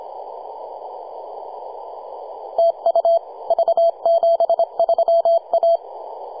Morse code, heard as a keyed tone, about 25 wpm, 670 Hz. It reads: TU V73A
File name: posts / 2015-sept-26-2040z-v73a-21-025-cw